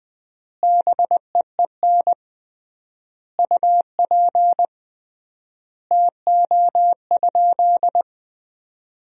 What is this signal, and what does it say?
Morse code, heard as a keyed tone, about 20 wpm, 700 Hz. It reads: BEEN UP TO?
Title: 3x Word-Spacing - Sentences from Top 100 Words - 20wpm